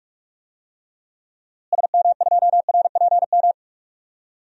Morse code, heard as a keyed tone, about 46 wpm, 700 Hz. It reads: SM2RPM
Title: DXCC-100 Call Signs - 6 Characters - Repeated 3x - 45wpm